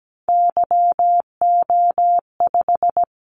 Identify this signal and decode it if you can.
Morse code, heard as a keyed tone, about 17 wpm, 700 Hz. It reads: YO5
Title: Callsign Prefixes for Top 25 DXCC - Repeated 3x - 17wpm